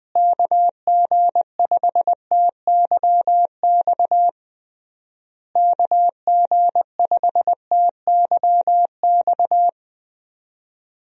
Morse code, heard as a keyed tone, about 20 wpm, 700 Hz. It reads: KG5TYX KG5TYX
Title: USA Call Signs - 6 Characters - Repeated 3x - 20wpm